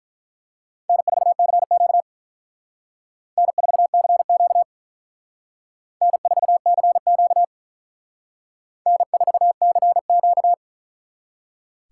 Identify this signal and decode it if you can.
Morse code, heard as a keyed tone, about 43 wpm, 700 Hz. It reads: N4CQ N4CQ N4CQ N4CQ
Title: Speed Racing - USA Call Signs 35wpm